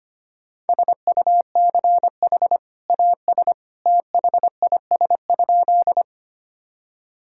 Morse code, heard as a keyed tone, about 25 wpm, 700 Hz. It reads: SUCH AS THIS?
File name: practice / 25wpm / Sentences with 2-Word Phrases - 25wpm